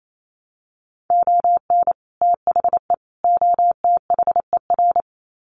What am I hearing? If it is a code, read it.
Morse code, heard as a keyed tone, about 28 wpm, 700 Hz. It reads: ON THE OTHER
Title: Sets of 3 Words 28wpm